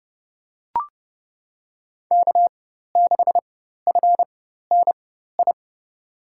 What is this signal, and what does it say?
Morse code, heard as a keyed tone, about 32 wpm, spaced as for 14 wpm, 700 Hz. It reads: K6FNI
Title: USA Call Signs - 5 Characters - 4x Character Spacing - 30wpm